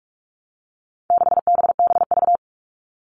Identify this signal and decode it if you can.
Morse code, heard as a keyed tone, about 44 wpm, 700 Hz. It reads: 6BBV
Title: All Cut Numbers - 4-Digit 45wpm